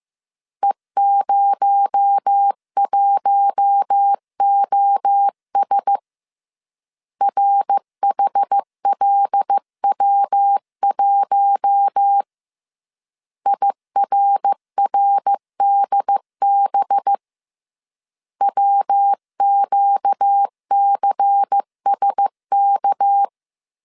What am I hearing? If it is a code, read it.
Morse code, heard as a keyed tone, about 15 wpm, 800 Hz. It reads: E01OS RHLW1 IRRDB WQCSK